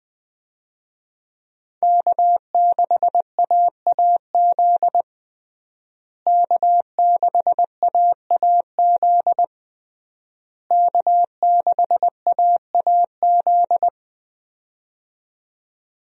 Morse code, heard as a keyed tone, about 20 wpm, 700 Hz. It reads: K6AAZ K6AAZ K6AAZ
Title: USA Call Signs - 5 Characters - Repeated 3x - 20wpm